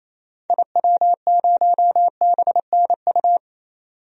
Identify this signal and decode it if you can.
Morse code, heard as a keyed tone, about 28 wpm, 700 Hz. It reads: IW0BNU
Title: DXCC-100 Call Signs - 6 Characters - 28wpm